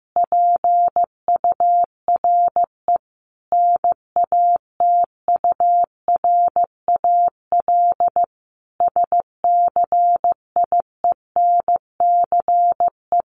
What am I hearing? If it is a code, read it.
Morse code, heard as a keyed tone, about 15 wpm, 700 Hz. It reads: PURE NATURAL SCIENCE
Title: Sentences from Top 1500 Words 15wpm